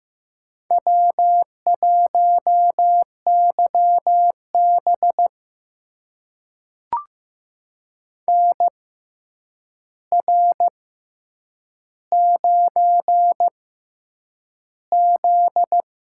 Morse code, heard as a keyed tone, about 15 wpm, spaced as for 8 wpm, 700 Hz. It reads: W1YB NR9Z
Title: USA Call Signs - 4 Characters - 6x Character Spacing - 15wpm